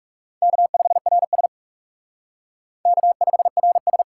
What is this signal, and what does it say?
Morse code, heard as a keyed tone, about 40 wpm, 700 Hz. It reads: K5RS K5RS